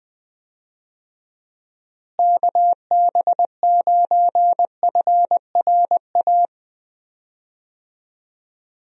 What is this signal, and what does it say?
Morse code, heard as a keyed tone, about 21 wpm, 700 Hz. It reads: KB9FRA
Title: USA Call Signs - 6 Characters - Repeated 3x - 20wpm